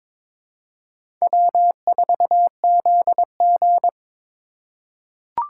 Morse code, heard as a keyed tone, about 22 wpm, 700 Hz. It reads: W4ZG
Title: USA Call Signs - 3x Character Spacing - 22wpm